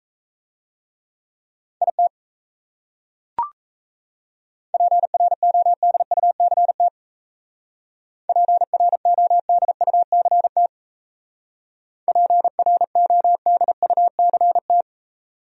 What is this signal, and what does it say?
Morse code, heard as a keyed tone, about 35 wpm, 700 Hz. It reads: IT PRODUCT PRODUCT PRODUCT